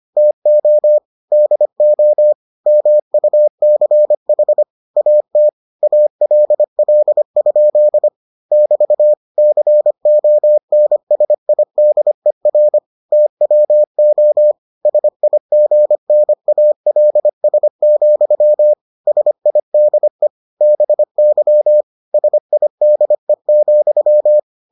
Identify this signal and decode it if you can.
Morse code, heard as a keyed tone, about 25 wpm, 600 Hz. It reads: TO DO MUCH AT ALL? <BT> CONSIDER TWO SIGNALS, SIDE BY SIDE,